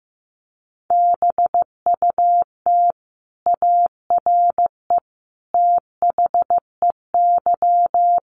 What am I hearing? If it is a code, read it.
Morse code, heard as a keyed tone, about 15 wpm, 700 Hz. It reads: BUT ARE THEY
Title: Sentences from Top 900 Words 15wpm